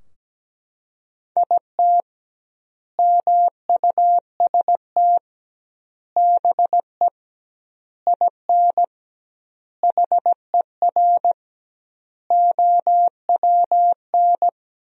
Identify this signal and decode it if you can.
Morse code, heard as a keyed tone, about 17 wpm, 700 Hz. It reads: IT MUST BE IN HER OWN
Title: 2x Word-Spacing - Sentences from Top 300 Words - 17wpm